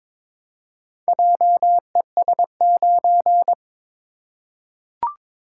Morse code, heard as a keyed tone, about 22 wpm, 700 Hz. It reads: JES9
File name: practice / 22wpm / Four Letter-Number Combinations - 22wpm